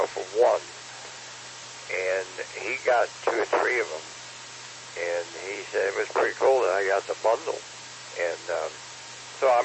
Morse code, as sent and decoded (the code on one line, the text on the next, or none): none